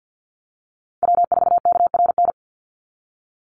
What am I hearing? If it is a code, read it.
Morse code, heard as a keyed tone, about 50 wpm, 700 Hz. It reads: W4KRN
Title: NA SOTA Chaser Callsigns 50wpm